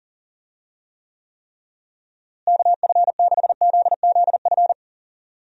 Morse code, heard as a keyed tone, about 37 wpm, 700 Hz. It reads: KF6ZZF
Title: USA Call Signs - 6 Characters - Repeated 3x - 40wpm